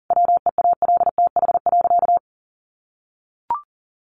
Morse code, heard as a keyed tone, about 40 wpm, 700 Hz. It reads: WEALTH.
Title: Sentences from Top 1200 Words 40wpm